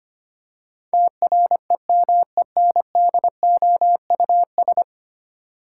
Morse code, heard as a keyed tone, about 25 wpm, 700 Hz. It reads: TREMENDOUS
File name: Ten Letter Words - 25wpm